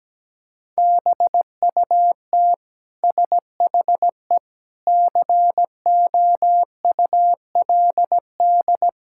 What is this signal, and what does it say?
Morse code, heard as a keyed tone, about 17 wpm, 700 Hz. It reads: BUT SHE COULD
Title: Sentences from Top 1000 Words - Review Part 6 - 17wpm